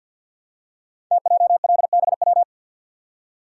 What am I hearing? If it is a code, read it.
Morse code, heard as a keyed tone, about 52 wpm, 700 Hz. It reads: TJLBW
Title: Code Groups - 5 Letter:Numbers - 50wpm